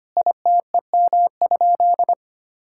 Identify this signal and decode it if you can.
Morse code, heard as a keyed tone, about 25 wpm, 700 Hz. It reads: ITEM?
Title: Sentences from Top 1000 Words - Review Part 6 - 25wpm